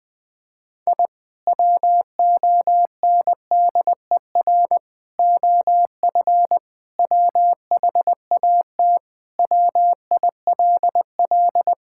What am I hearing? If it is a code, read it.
Morse code, heard as a keyed tone, about 20 wpm, 700 Hz. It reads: I WONDER OF WHAT WILL
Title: Sentences with 2-Word Phrases - Spoken Beginning - 20wpm